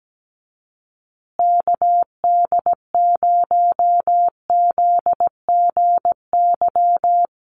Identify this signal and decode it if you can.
Morse code, heard as a keyed tone, about 17 wpm, 700 Hz. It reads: KD0ZGY
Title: USA Call Signs Repeated 3x 17wpm